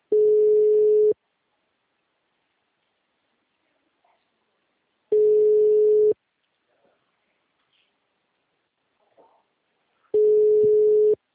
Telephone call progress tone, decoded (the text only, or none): ringback tone